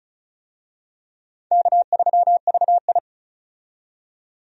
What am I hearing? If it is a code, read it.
Morse code, heard as a keyed tone, about 36 wpm, 700 Hz. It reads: K3VI
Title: USA Call Signs - 4 Characters - 35wpm